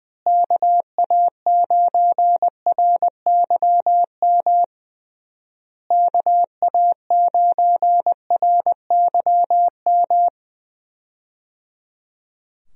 Morse code, heard as a keyed tone, about 20 wpm, 700 Hz. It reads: KA9RYM KA9RYM
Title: USA Call Signs - 6 Characters - Repeated 3x - 20wpm